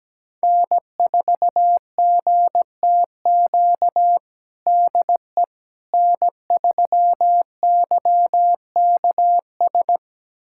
Morse code, heard as a keyed tone, about 17 wpm, 700 Hz. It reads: N4GTQ DE N3YKS
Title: USA Callsign DE Callsign - 5 Characters - 17wpm